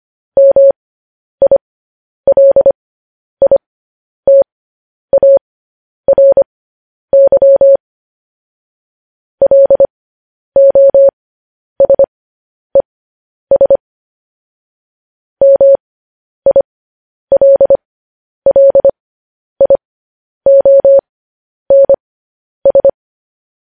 Morse code, heard as a keyed tone, about 25 wpm, spaced as for 10 wpm, 550 Hz. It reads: MILITARY LOSES MILLIONS